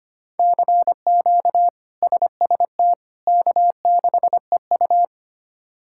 Morse code, heard as a keyed tone, about 25 wpm, 700 Hz. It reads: CQ SST K6EU
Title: K1USN SST QSO Practice - 25wpm